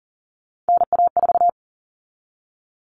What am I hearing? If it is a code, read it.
Morse code, heard as a keyed tone, about 40 wpm, 700 Hz. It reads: NA4
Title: All Cut Numbers - 3-Digit 40wpm